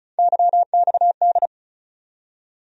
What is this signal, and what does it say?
Morse code, heard as a keyed tone, about 36 wpm, 700 Hz. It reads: YXD
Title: USA Call Sign Postfixes - 3 Letters 35wpm